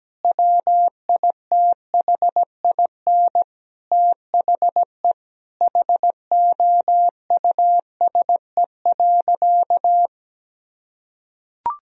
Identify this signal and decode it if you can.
Morse code, heard as a keyed tone, about 17 wpm, 700 Hz. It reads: WITHIN THE HOUSE.
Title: Sentences from Top 1000 Words - Review Part 4 - 17wpm